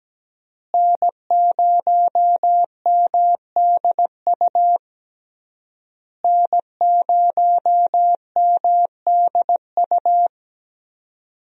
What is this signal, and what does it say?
Morse code, heard as a keyed tone, about 17 wpm, 700 Hz. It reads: N0MDU N0MDU
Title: USA Call Signs - 5 Characters - Repeated 3x - 17wpm